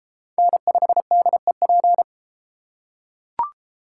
Morse code, heard as a keyed tone, about 33 wpm, 700 Hz. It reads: N5DEP